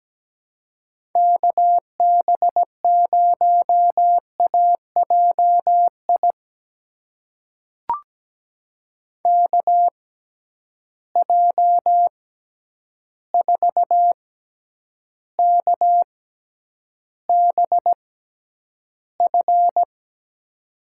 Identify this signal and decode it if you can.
Morse code, heard as a keyed tone, about 17 wpm, spaced as for 9 wpm, 700 Hz. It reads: KB0AJI KJ4KBF
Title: USA Call Signs - 6 Characters - 6x Character Spacing - 17wpm